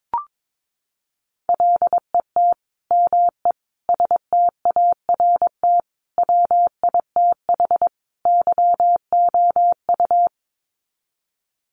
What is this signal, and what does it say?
Morse code, heard as a keyed tone, about 22 wpm, 700 Hz. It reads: LET ME START WITH YOU
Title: Sets of 5 Words Encore 22wpm